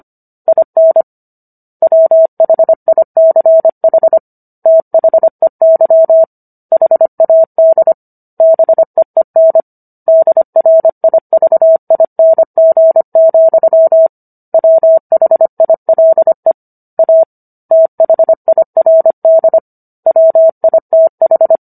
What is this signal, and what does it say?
Morse code, heard as a keyed tone, about 25 wpm, 650 Hz. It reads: IN WHICH THEY HAD BEEN DRIVING, WHILE A THIRD WITH